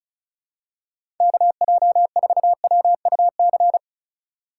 Morse code, heard as a keyed tone, about 36 wpm, 700 Hz. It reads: KJ4WUC
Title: USA Call Signs - 4x Character Spacing - 35wpm